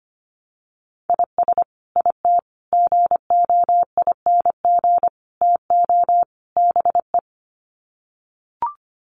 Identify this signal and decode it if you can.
Morse code, heard as a keyed tone, about 25 wpm, 700 Hz. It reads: IS IT GOING TO BE
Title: Sets of 5 Words Encore2 25wpm